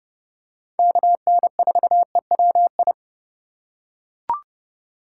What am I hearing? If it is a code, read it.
Morse code, heard as a keyed tone, about 30 wpm, 700 Hz. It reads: KN4EWI
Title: K1USN SST QSO Practice - 30wpm